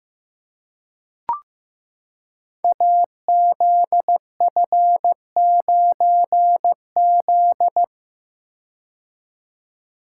Morse code, heard as a keyed tone, about 15 wpm, 700 Hz. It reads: AZF9Z
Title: Code Groups - 5 Letter:Numbers - 15wpm